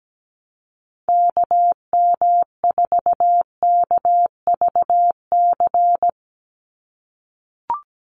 Morse code, heard as a keyed tone, about 17 wpm, 700 Hz. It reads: KM4KVC
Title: USA Call Signs 17wpm